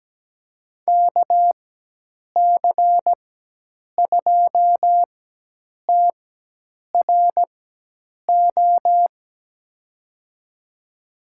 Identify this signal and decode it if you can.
Morse code, heard as a keyed tone, about 17 wpm, spaced as for 8 wpm, 700 Hz. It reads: KC2TRO
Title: USA Call Signs - 6 Characters - 4x Character Spacing - 17wpm